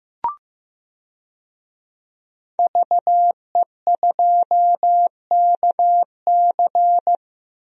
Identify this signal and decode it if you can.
Morse code, heard as a keyed tone, about 15 wpm, 700 Hz. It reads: VE2KC
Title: DXCC-100 Call Signs - 5 Characters - Repeated 3x - 15wpm